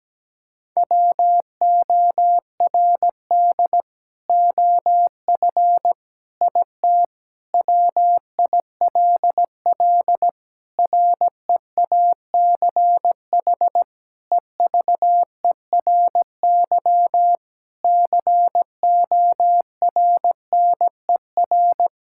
Morse code, heard as a keyed tone, about 17 wpm, 700 Hz. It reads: WORD OF IT WILL REACH EVERY CORNER
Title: Sentences from Top 1600 Words 17wpm